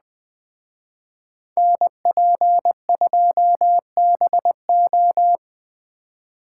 Morse code, heard as a keyed tone, about 20 wpm, 700 Hz. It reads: NP2BO
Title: Speed Racing - USA 2x2 Call Signs 17wpm